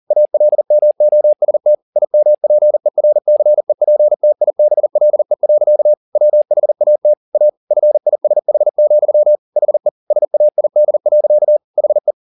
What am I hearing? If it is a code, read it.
Morse code, heard as a keyed tone, about 40 wpm, 600 Hz. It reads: ALMOST IMPERCEPTIBLE. WHAT A FISH, HE SAID. HE